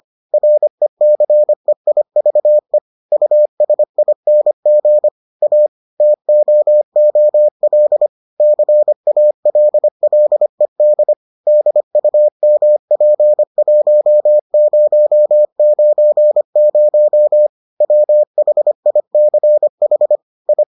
Morse code, heard as a keyed tone, about 25 wpm, 600 Hz. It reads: RECEIVE USING A TOOL CALLED DUMP1090 WHICH I